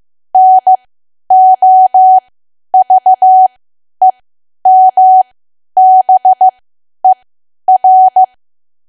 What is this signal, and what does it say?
Morse code, heard as a keyed tone, about 15 wpm, spaced as for 10 wpm, 750 Hz. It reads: NOVEMBER
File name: cwa / W1AW-Level2 / 10WPM